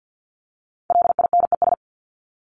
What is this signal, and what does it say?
Morse code, heard as a keyed tone, about 50 wpm, 700 Hz. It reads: LINES
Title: ES:S Suffix - Top 100 Words with Base Words 50wpm